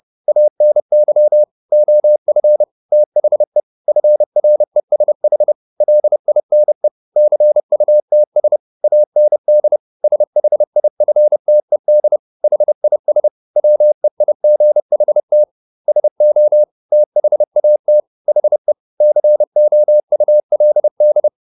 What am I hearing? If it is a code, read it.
Morse code, heard as a keyed tone, about 30 wpm, 600 Hz. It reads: ANY OF THE FRESH LINE CUTS AND SHIFTED HIS WEIGHT SO THAT HE COULD